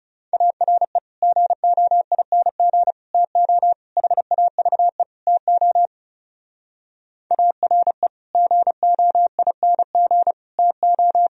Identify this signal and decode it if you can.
Morse code, heard as a keyed tone, about 40 wpm, 700 Hz. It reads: ARE GOING TO HAVE TO ARE GOING TO